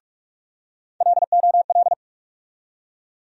Morse code, heard as a keyed tone, about 46 wpm, 700 Hz. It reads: LOL
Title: USA Call Sign Postfixes - 3 Letters 45wpm